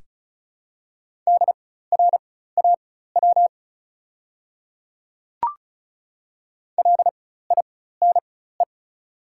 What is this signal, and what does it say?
Morse code, heard as a keyed tone, about 36 wpm, spaced as for 16 wpm, 700 Hz. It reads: DRAW LINE
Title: 4x Character Spacing - Top 300 Words - 35wpm